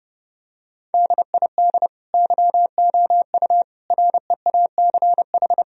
Morse code, heard as a keyed tone, about 32 wpm, 700 Hz. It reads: DID YOU REACH
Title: Sentences from Top 1000 Words - Review Part 3 - 30wpm